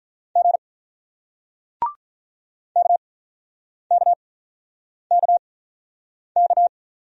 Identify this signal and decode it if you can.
Morse code, heard as a keyed tone, about 47 wpm, spaced as for 9 wpm, 700 Hz. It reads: G KKKK